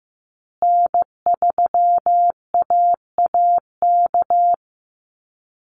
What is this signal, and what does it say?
Morse code, heard as a keyed tone, about 15 wpm, 700 Hz. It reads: N3AAK
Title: Call Signs Repeated 3x - 15wpm